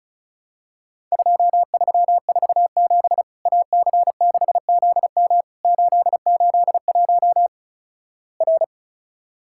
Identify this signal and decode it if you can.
Morse code, heard as a keyed tone, about 36 wpm, 700 Hz. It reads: 2347 AC6ZM 881 R
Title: Worked All Europe DX Contest - QTC Exchange - 35wpm